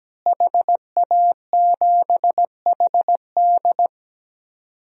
Morse code, heard as a keyed tone, about 17 wpm, 700 Hz. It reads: HA7HD
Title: DXCC-100 Call Signs - 5 Characters - Repeated 3x - 17wpm